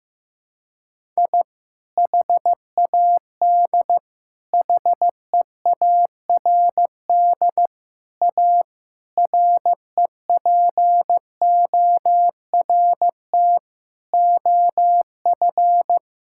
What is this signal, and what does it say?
Morse code, heard as a keyed tone, about 16 wpm, 700 Hz. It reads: I HAD HEARD A REPORT OF
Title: Sentences from Top 1000 Words - Review Part 6 - 15wpm